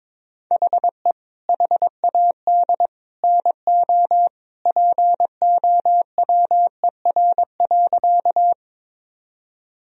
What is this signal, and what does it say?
Morse code, heard as a keyed tone, about 22 wpm, 700 Hz. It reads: HE HAD NO POWER.
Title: Sentences from Top 1000 Words - Review Part 2 - 22wpm